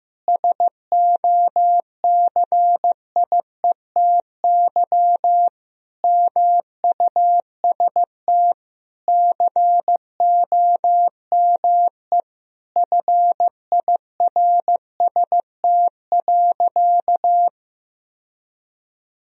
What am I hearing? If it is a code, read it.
Morse code, heard as a keyed tone, about 15 wpm, 700 Hz. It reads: SOCIETY MUST COME FIRST.
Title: Sentences from Top 1000 Words - Review Part 4 - 15wpm